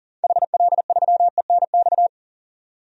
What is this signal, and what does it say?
Morse code, heard as a keyed tone, about 37 wpm, 700 Hz. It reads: HL3ENX
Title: DXCC-100 Call Signs - 6 Characters - 40wpm